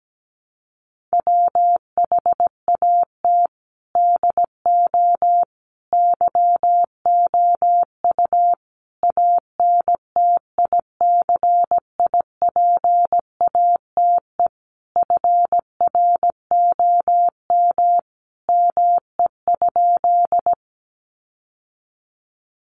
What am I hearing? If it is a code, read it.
Morse code, heard as a keyed tone, about 17 wpm, 700 Hz. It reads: WHAT DO YOU ANTICIPATE FROM ME?
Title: Sentences from Top 1500 Words 17wpm